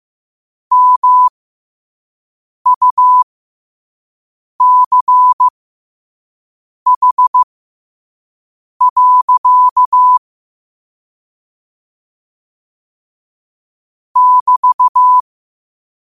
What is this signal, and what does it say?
Morse code, heard as a keyed tone, about 15 wpm, spaced as for 5 wpm, 1000 Hz. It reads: MUCH. <BT>
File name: qotdfiles / 5wpm